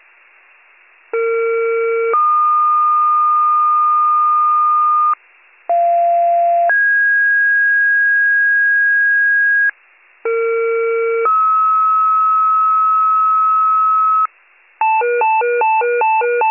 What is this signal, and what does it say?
Two-tone page, heard as a one-second tone, then a three-second tone, about 460 then 1220 Hz, about 690 then 1650 Hz, about 460 then 1290 Hz. It is heard from 1.1 to 14.3 s.